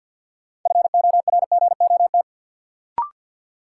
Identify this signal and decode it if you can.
Morse code, heard as a keyed tone, about 53 wpm, 700 Hz. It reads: FORGOT
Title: Sentences from Top 1400 Words - New Word Drill Alt - 50wpm